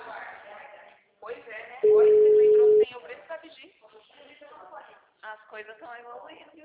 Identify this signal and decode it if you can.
Ringback tone (Europe, 425 Hz), with speech.
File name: out- -2002- - - .22482